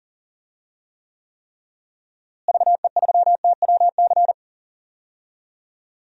Morse code, heard as a keyed tone, about 40 wpm, 700 Hz. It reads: VE3TWC
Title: DXCC-100 Call Signs - 6 Characters - Repeated 3x - 40wpm